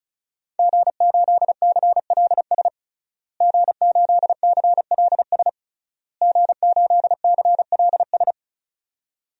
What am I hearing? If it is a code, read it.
Morse code, heard as a keyed tone, about 34 wpm, 700 Hz. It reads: G8CLS G8CLS G8CLS
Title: DXCC-100 Call Signs - 5 Characters - Repeated 3x - 35wpm